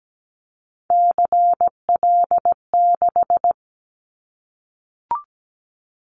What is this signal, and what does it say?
Morse code, heard as a keyed tone, about 17 wpm, 700 Hz. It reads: CL6
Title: Callsign Prefixes for Top 100 DXCC - 17wpm